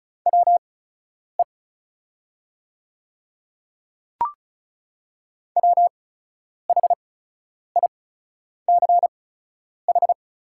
Morse code, heard as a keyed tone, about 36 wpm, spaced as for 10 wpm, 700 Hz. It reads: WE WHICH
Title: 8x Character Spacing - Top 100 Words - 35wpm